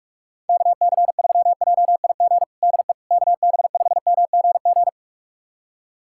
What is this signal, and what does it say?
Morse code, heard as a keyed tone, about 42 wpm, 700 Hz. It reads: KC3JIG DE KB5MGG
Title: USA Callsign DE Callsign 45wpm